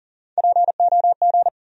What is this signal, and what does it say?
Morse code, heard as a keyed tone, about 40 wpm, 700 Hz. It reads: POG
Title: USA Call Sign Postfixes - 3 Letters 40wpm